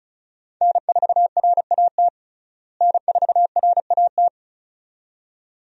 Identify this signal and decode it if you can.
Morse code, heard as a keyed tone, about 36 wpm, 700 Hz. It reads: N4RAT N4RAT
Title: USA Call Signs - 5 Characters - Repeated 2x - 35wpm